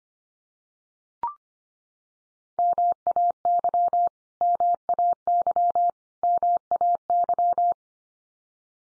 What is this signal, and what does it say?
Morse code, heard as a keyed tone, about 25 wpm, 700 Hz. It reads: MAY MAY MAY
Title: Top 100 Words 25wpm x3